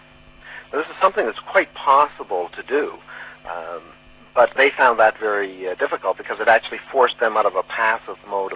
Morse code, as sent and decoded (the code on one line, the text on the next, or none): none